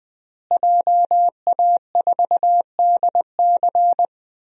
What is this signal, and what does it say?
Morse code, heard as a keyed tone, about 20 wpm, 700 Hz. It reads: JA4DC
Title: DXCC-100 Call Signs - 5 Characters - Repeated 2x - 20wpm